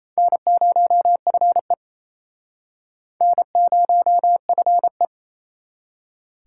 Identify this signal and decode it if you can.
Morse code, heard as a keyed tone, about 37 wpm, 700 Hz. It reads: N0FE N0FE